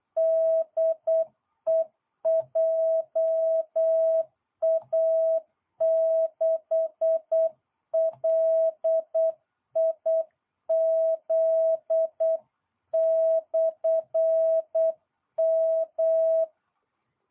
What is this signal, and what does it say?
Morse code, heard as a keyed tone, about 8 wpm, 660 Hz. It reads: DE JA6LIZ/M